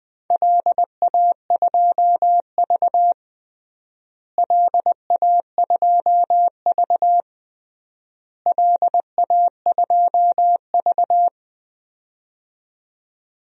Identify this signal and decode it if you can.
Morse code, heard as a keyed tone, about 20 wpm, 700 Hz. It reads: LA2V LA2V LA2V
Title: DXCC-100 Call Signs - 4 Characters - Repeated 3x - 20wpm